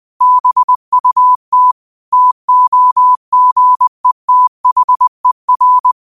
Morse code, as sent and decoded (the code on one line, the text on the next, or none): -... ..- - / - --- --. - .... .-.
BUT TOGETHER